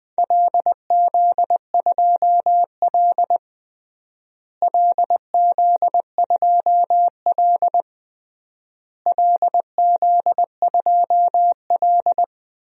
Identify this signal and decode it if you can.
Morse code, heard as a keyed tone, about 20 wpm, 700 Hz. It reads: LZ2L LZ2L LZ2L
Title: DXCC-100 Call Signs - 4 Characters - Repeated 3x - 20wpm